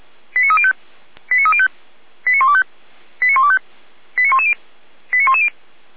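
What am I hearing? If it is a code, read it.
Five-tone selective call (ZVEI), heard as quick runs of five tones, tones of about 70 ms. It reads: 89305 89305 89125 89125 89110 89110
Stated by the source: ZVEI five-tone call received on an SDR.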